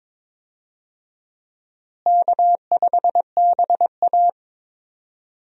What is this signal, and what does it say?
Morse code, heard as a keyed tone, about 22 wpm, 700 Hz. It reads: K5BA
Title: USA Call Signs - 4 Characters - Repeated 2x - 22wpm